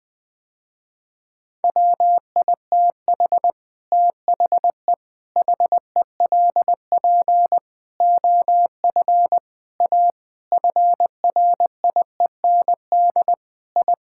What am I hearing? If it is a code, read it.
Morse code, heard as a keyed tone, about 21 wpm, 700 Hz. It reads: WITH THE HELP OF A FRIEND I
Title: Sentences with 5-Word Phrases - Spoken Beginning - 20wpm